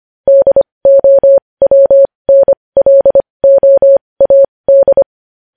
Morse code, heard as a keyed tone, about 25 wpm, spaced as for 20 wpm, 550 Hz. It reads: DOWNLOAD
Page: news headlines in morse code at 20 wpm